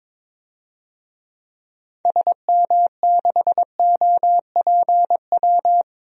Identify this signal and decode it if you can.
Morse code, heard as a keyed tone, about 22 wpm, 700 Hz. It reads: SM6OPW
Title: DXCC-100 Call Signs - 6 Characters - 22wpm